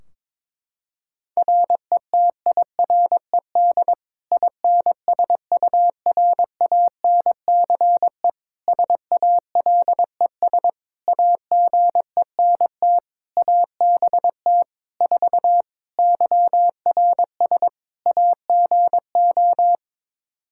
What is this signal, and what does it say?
Morse code, heard as a keyed tone, about 22 wpm, 700 Hz. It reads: RETIRED INSURANCE SALES AGENT ABT 4 YRS AGO
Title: K1USN SST QSO Practice 22wpm